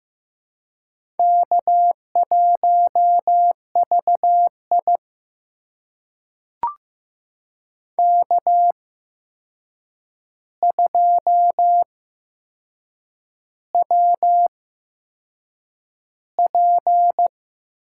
Morse code, heard as a keyed tone, about 15 wpm, spaced as for 6 wpm, 700 Hz. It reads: K1VI K2WP
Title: USA Call Signs - 4 Characters - 8x Character Spacing - 15wpm